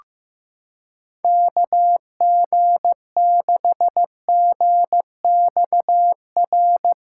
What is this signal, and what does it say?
Morse code, heard as a keyed tone, about 15 wpm, 700 Hz. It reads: KG6GXR
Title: USA Callsign DE Callsign 15wpm